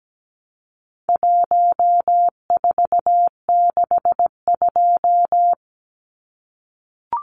Morse code, heard as a keyed tone, about 17 wpm, 700 Hz. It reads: 1462
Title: Numbers - 4-Digit 17wpm